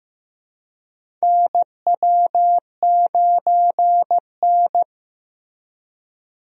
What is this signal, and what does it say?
Morse code, heard as a keyed tone, about 15 wpm, 700 Hz. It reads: NW9N
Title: USA Call Signs - 4 Characters - Repeated 3x - 15wpm